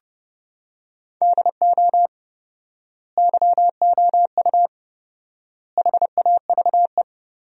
Morse code, heard as a keyed tone, about 30 wpm, 700 Hz. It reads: DO YOU HAVE
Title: 4x Word-Spacing - Sentences from Top 500 Words - 30wpm